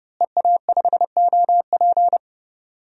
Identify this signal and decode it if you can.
Morse code, heard as a keyed tone, about 30 wpm, 700 Hz. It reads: EA5OP
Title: DXCC-100 Call Signs - 5 Characters - Repeated 2x - 30wpm